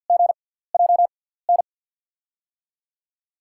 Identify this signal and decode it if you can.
Morse code, heard as a keyed tone, about 53 wpm, 700 Hz. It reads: G J N
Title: USA Call Signs - 6 Characters - 6x Character Spacing - 50wpm